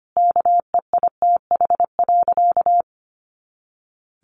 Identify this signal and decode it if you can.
Morse code, heard as a keyed tone, about 25 wpm, 700 Hz. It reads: KEITH.
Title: A Sound Of Thunder 25wpm